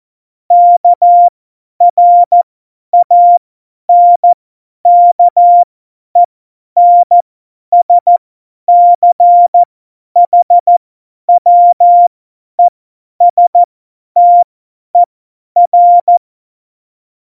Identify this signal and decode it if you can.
Morse code, heard as a keyed tone, about 14 wpm, spaced as for 10 wpm, 700 Hz. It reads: KRANKENSCHWESTER
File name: Morsekurs / Witze / Witz 15. September 2020, 50 F70